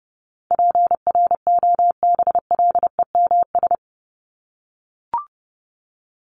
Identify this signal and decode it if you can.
Morse code, heard as a keyed tone, about 30 wpm, 700 Hz. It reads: PROBLEMS
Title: ES:S Suffix - Top 100 Words with Base Words 30wpm